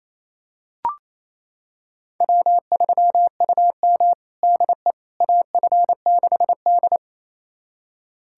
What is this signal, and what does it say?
Morse code, heard as a keyed tone, about 28 wpm, 690 Hz. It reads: W3UM DE AF6D
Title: USA Callsign DE Callsign - 4 Characters - 28wpm